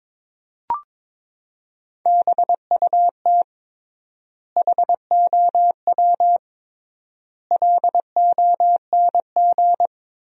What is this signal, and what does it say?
Morse code, heard as a keyed tone, about 22 wpm, 700 Hz. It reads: BUT HOW LONG
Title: 3x Word-Spacing - Sentences from Top 500 Words - 22wpm